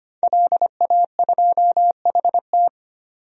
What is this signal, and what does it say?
Morse code, heard as a keyed tone, about 25 wpm, 700 Hz. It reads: LA2HT
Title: DXCC-100 Call Signs - 5 Characters - 25wpm